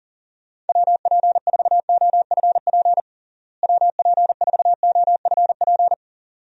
Morse code, heard as a keyed tone, about 40 wpm, 700 Hz. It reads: WP4OFP WP4OFP
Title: USA Call Signs - 6 Characters - Repeated 2x - 40wpm